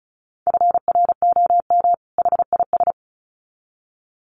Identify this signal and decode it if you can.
Morse code, heard as a keyed tone, about 34 wpm, 700 Hz. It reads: FROM HIS
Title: Sets of 2 Words 35wpm